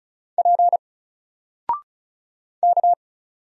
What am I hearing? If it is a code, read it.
Morse code, heard as a keyed tone, about 36 wpm, 700 Hz. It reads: P K
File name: Single-Letter-Number-35wpm